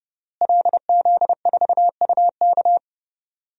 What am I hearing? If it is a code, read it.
Morse code, heard as a keyed tone, about 30 wpm, 700 Hz. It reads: LZ4UK
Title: DXCC-100 Call Signs - 5 Characters - Repeated 3x - 30wpm